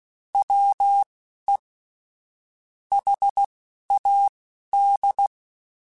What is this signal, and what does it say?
Morse code, heard as a keyed tone, about 16 wpm, spaced as for 11 wpm, 800 Hz. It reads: WE HAD
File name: Weather Sets / Weather Sets-mp3s / Weather 4 24 @10